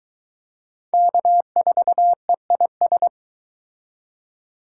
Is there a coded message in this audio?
Yes: Morse code from 0.9 to 3.1 s, heard as a keyed tone, about 23 wpm, 700 Hz. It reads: K4EIS